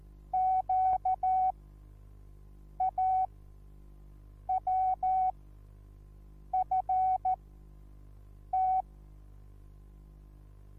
Morse code, heard as a keyed tone, about 14 wpm, spaced as for 6 wpm, 750 Hz. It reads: QAWFT
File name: Onlinelehrgang / morsen / DJ4UF Morsen21